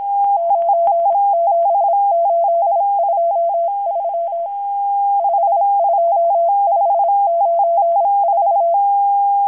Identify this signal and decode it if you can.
Morse code, heard as a keyed tone, about 26 wpm, 680 Hz. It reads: 97823 52594
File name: HM02- -0437utc-7351khz